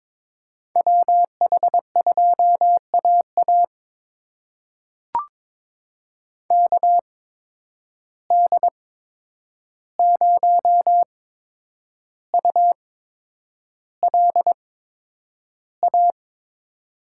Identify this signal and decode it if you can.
Morse code, heard as a keyed tone, about 23 wpm, spaced as for 9 wpm, 700 Hz. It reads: WH2AA KD0ULA
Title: USA Call Signs - 8x Character Spacing - 22wpm